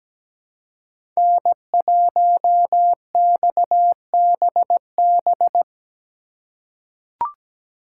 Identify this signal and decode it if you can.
Morse code, heard as a keyed tone, about 17 wpm, 700 Hz. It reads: N1XBB
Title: USA Call Signs - 5 Characters - 4x Character Spacing - 17wpm